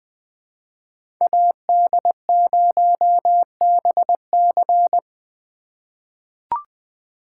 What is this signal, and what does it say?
Morse code, heard as a keyed tone, about 20 wpm, 700 Hz. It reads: AD0BC
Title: 2023 POTA Hunter Call Signs - 20wpm